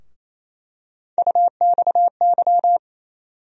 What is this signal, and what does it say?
Morse code, heard as a keyed tone, about 28 wpm, 700 Hz. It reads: UXY
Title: USA Call Sign Postfixes - 3 Letters 28wpm